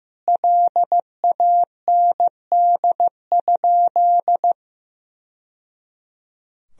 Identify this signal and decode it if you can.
Morse code, heard as a keyed tone, about 15 wpm, 700 Hz. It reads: LAND?
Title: Sentences from Top 1000 Words - Review Part 1 - 15wpm